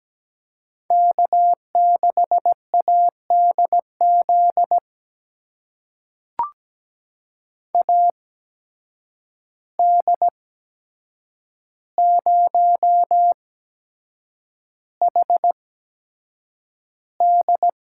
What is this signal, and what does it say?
Morse code, heard as a keyed tone, about 17 wpm, spaced as for 7 wpm, 700 Hz. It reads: K6ADZ AD0HD
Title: USA Call Signs - 5 Characters - 8x Character Spacing - 17wpm